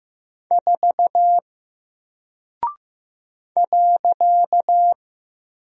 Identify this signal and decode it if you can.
Morse code, heard as a keyed tone, about 15 wpm, 700 Hz. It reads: 4 .
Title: Single Character (Letter:Number:Punctuation:Prosign) Alt 15wpm